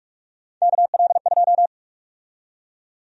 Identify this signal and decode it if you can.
Morse code, heard as a keyed tone, about 41 wpm, 700 Hz. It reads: KL2
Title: USA Call Sign Prefixes - 45wpm